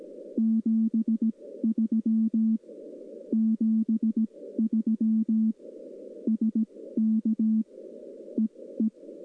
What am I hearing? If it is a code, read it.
Morse code, heard as a keyed tone, about 17 wpm, spaced as for 14 wpm, 220 Hz. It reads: 73 73 SK EE
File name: contests / 2019-june=vhf-k5tr- -0101 0-2